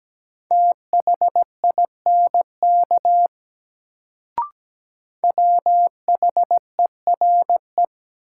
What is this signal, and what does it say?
Morse code, heard as a keyed tone, about 17 wpm, 700 Hz. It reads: THINK WHERE
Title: Sentences from Top 200 Words - New Word Drill - 17wpm